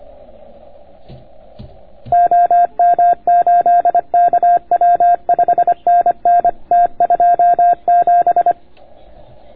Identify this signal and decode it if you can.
Morse code, heard as a keyed tone, about 25 wpm, 690 Hz. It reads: OM8KW 5NN T27